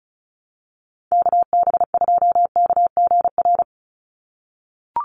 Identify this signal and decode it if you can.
Morse code, heard as a keyed tone, about 34 wpm, 700 Hz. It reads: KB2KGR
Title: USA Call Signs 35wpm